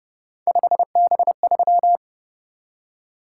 Morse code, heard as a keyed tone, about 29 wpm, 700 Hz. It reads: 5B3